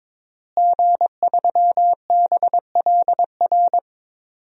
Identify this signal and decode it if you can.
Morse code, heard as a keyed tone, about 22 wpm, 700 Hz. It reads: G3BLR
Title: DXCC-100 Call Signs - 5 Characters - Repeated 3x - 22wpm